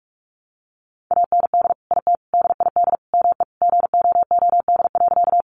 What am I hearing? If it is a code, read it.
Morse code, heard as a keyed tone, about 44 wpm, 700 Hz. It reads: AND IT DID ME GOOD.
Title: Sentences from Top 200 Words 45wpm